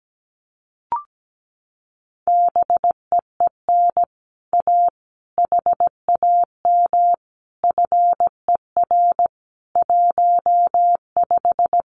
Morse code, heard as a keyed tone, about 17 wpm, 700 Hz. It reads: BEEN A HAM FER 15
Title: Common QSO Phrases Encore 17wpm